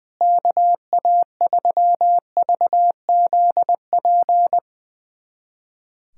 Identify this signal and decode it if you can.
Morse code, heard as a keyed tone, about 21 wpm, 700 Hz. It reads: KA3VZP
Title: USA Callsign DE Callsign - 6 Characters - 20wpm